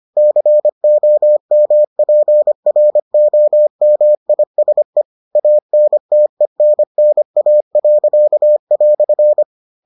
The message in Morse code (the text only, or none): COMPROMISE ANTENNA."